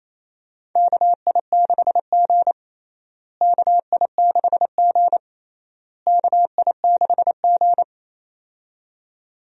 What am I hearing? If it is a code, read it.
Morse code, heard as a keyed tone, about 28 wpm, 700 Hz. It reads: KI6G KI6G KI6G
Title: USA Call Signs - 4 Characters - Repeated 3x - 28wpm